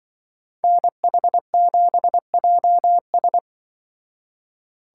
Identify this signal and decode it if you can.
Morse code, heard as a keyed tone, about 24 wpm, 700 Hz. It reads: NH7JS
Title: Speed Racing - USA 2x2 Call Signs 20wpm